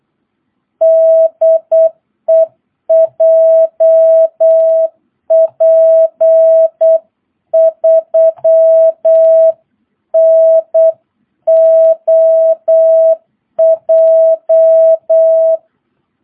Morse code, heard as a keyed tone, about 8 wpm, 650 Hz. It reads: DE JP3NOJ
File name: jh8cbh / 2022cw